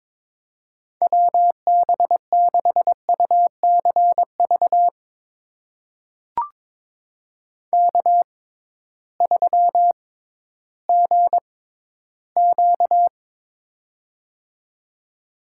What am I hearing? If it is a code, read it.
Morse code, heard as a keyed tone, about 22 wpm, spaced as for 13 wpm, 700 Hz. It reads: WB6UCV K3GQ
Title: USA Call Signs - 6x Character Spacing - 22wpm